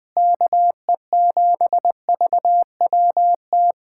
Morse code, heard as a keyed tone, about 20 wpm, 700 Hz. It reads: KE7VWT
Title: USA Callsign DE Callsign - 6 Characters - 20wpm